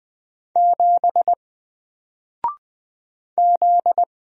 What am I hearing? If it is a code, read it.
Morse code, heard as a keyed tone, about 20 wpm, 700 Hz. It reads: 7 Z
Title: Single Letter-Number 20wpm